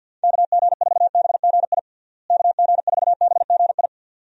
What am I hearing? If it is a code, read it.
Morse code, heard as a keyed tone, about 52 wpm, 700 Hz. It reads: KG4BGI KG4BGI
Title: USA Call Signs - 6 Characters - Repeated 2x - 50wpm